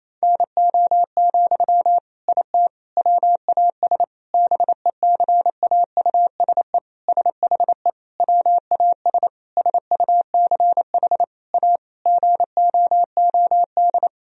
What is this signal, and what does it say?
Morse code, heard as a keyed tone, about 28 wpm, 700 Hz. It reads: NO, IT WAS BECAUSE SHE WAS SUCH A GOOD
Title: Sentences from Top 1000 Words - Review Part 4 - 28wpm